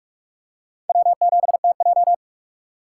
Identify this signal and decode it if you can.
Morse code, heard as a keyed tone, about 42 wpm, 700 Hz. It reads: W7TJ
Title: USA Call Signs - 4 Characters - 6x Character Spacing - 45wpm